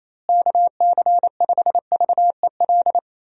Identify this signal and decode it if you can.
Morse code, heard as a keyed tone, about 27 wpm, 700 Hz. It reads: KC5VEL
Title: USA Call Signs - 6 Characters - Repeated 3x - 28wpm